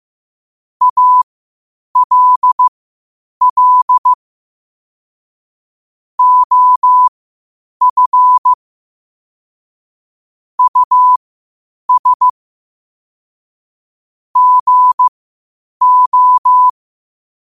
Morse code, heard as a keyed tone, about 15 wpm, spaced as for 8 wpm, 1000 Hz. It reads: ALL OF US GO
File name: qotdfiles / 8wpm